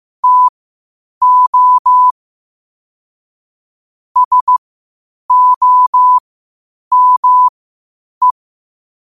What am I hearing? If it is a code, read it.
Morse code, heard as a keyed tone, about 15 wpm, spaced as for 8 wpm, 1000 Hz. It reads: TO SOME
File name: qotdfiles / 8wpm